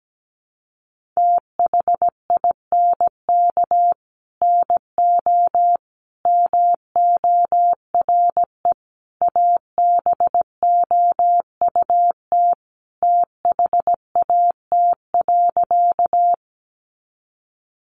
Morse code, heard as a keyed tone, about 17 wpm, 700 Hz. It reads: THINK NO MORE ABOUT THAT.
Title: Sentences from Top 200 Words 17wpm